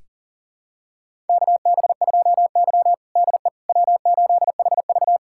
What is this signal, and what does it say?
Morse code, heard as a keyed tone, about 40 wpm, 700 Hz. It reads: KB2Y DE W8HV
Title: USA Callsign DE Callsign - 4 Characters - 40wpm